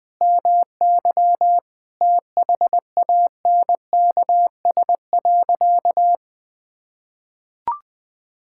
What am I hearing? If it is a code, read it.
Morse code, heard as a keyed tone, about 20 wpm, 700 Hz. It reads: MY THANKS.
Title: Sentences with 2-Word Phrases - 20wpm